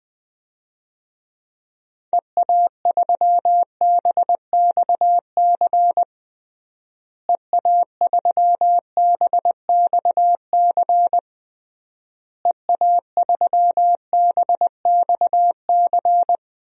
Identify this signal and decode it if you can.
Morse code, heard as a keyed tone, about 21 wpm, 700 Hz. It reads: EA3BXC EA3BXC EA3BXC
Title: DXCC-100 Call Signs - 6 Characters - Repeated 3x - 20wpm